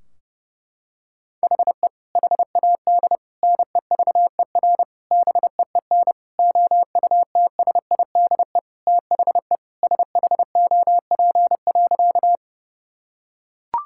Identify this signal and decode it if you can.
Morse code, heard as a keyed tone, about 30 wpm, 700 Hz. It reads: HE HAD NEVER BEEN OUTSIDE THE SHOP.
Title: Sentences from Top 1600 Words 30wpm